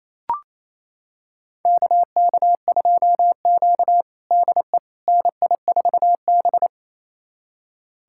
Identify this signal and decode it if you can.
Morse code, heard as a keyed tone, about 28 wpm, 700 Hz. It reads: KK2Q DE NI4B
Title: USA Callsign DE Callsign - 4 Characters - 28wpm